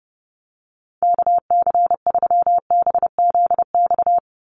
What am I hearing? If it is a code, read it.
Morse code, heard as a keyed tone, about 30 wpm, 700 Hz. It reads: KC3BZX
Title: USA Call Signs 30wpm